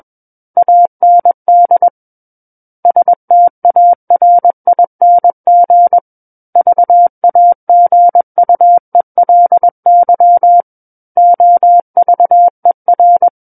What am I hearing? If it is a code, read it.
Morse code, heard as a keyed tone, about 21 wpm, 700 Hz. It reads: AND STARING VAGUELY OVER
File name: Book One / 12-Chapter 12 - 21 WPM